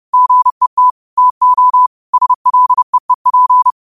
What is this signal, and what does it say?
Morse code, heard as a keyed tone, about 30 wpm, 1000 Hz. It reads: GET TO SLEEP